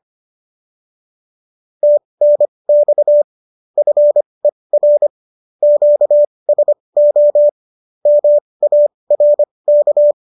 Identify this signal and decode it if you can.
Morse code, heard as a keyed tone, about 25 wpm, spaced as for 20 wpm, 590 Hz. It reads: TNX FER QSO MARK